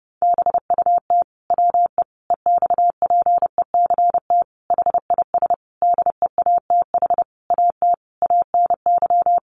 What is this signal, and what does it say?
Morse code, heard as a keyed tone, about 30 wpm, 700 Hz. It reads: BUT WE EXPECT HIS DEATH AT ANY
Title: Sentences from Top 1100 Words 30wpm